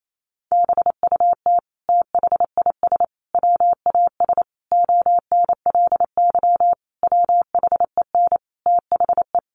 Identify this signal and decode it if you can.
Morse code, heard as a keyed tone, about 28 wpm, 700 Hz. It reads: BUT THIS WAS ONLY WHEN THE
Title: Sentences from Top 1300 Words 28wpm